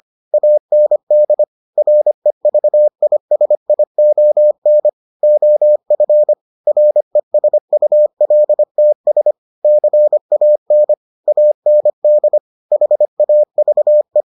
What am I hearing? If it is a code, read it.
Morse code, heard as a keyed tone, about 25 wpm, 600 Hz. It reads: AND REVISION OF RESULTS CAN AND HAVE